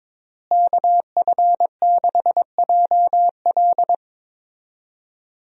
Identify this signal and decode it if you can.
Morse code, heard as a keyed tone, about 22 wpm, 700 Hz. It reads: KF6JL
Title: USA Call Signs - 5 Characters - Repeated 3x - 22wpm